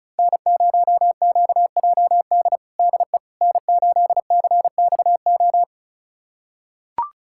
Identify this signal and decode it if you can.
Morse code, heard as a keyed tone, about 34 wpm, 700 Hz. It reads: N0QJD DE N8CXO
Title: USA Callsign DE Callsign - 5 Characters - 35wpm